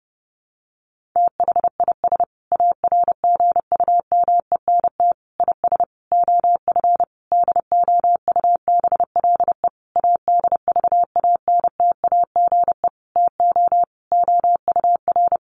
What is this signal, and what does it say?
Morse code, heard as a keyed tone, about 30 wpm, 700 Hz. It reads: THIS ARGUMENT IS OF DOUBLE ADVANTAGE TO OUR
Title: Sentences from Top 1400 Words 30wpm